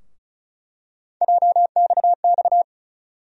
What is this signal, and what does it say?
Morse code, heard as a keyed tone, about 36 wpm, 700 Hz. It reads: JXX
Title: USA Call Sign Postfixes - 3 Letters 35wpm